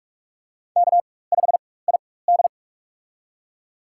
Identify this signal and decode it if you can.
Morse code, heard as a keyed tone, about 43 wpm, spaced as for 21 wpm, 700 Hz. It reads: K5ID
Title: USA Call Signs - 4 Characters - 4x Character Spacing - 45wpm